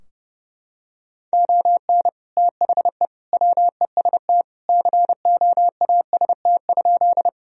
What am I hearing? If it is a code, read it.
Morse code, heard as a keyed tone, about 30 wpm, 700 Hz. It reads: ON THE WEST COAST?
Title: Sentences from Top 1000 Words - Review Part 4 - 30wpm